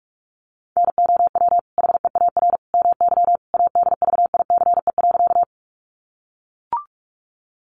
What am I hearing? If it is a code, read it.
Morse code, heard as a keyed tone, about 44 wpm, 700 Hz. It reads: NOW HEAR MY ADVICE.
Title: Sentences from Top 1100 Words 45wpm